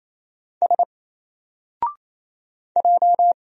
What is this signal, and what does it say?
Morse code, heard as a keyed tone, about 28 wpm, 700 Hz. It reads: S J
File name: Single-Letter-Number-28wpm